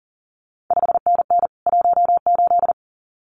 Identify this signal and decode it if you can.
Morse code, heard as a keyed tone, about 39 wpm, 700 Hz. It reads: 5NN 18
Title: CQ WW DX Contest Exchange Encore - 40wpm